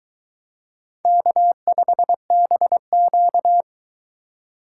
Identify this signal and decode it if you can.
Morse code, heard as a keyed tone, about 23 wpm, 700 Hz. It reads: K5BQ